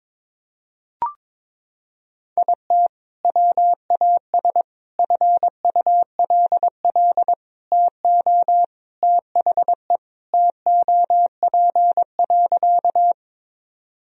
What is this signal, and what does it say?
Morse code, heard as a keyed tone, about 22 wpm, 700 Hz. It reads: IT WAS FULL TO THE TOP.
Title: Sentences from Top 1000 Words - Review Part 3 - 22wpm